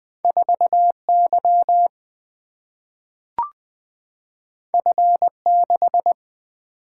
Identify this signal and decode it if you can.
Morse code, heard as a keyed tone, about 20 wpm, 700 Hz. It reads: 4 Y F 6
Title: Two Letter-Number Combinations - 20wpm